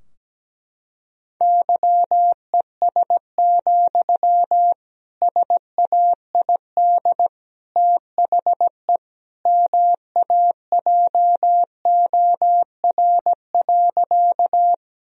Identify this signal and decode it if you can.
Morse code, heard as a keyed tone, about 17 wpm, 700 Hz. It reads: YES, SAID THE MAJOR.
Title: Sentences from Top 1000 Words - Review Part 4 - 17wpm